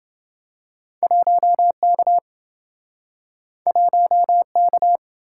Morse code, heard as a keyed tone, about 28 wpm, 700 Hz. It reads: 1K 1K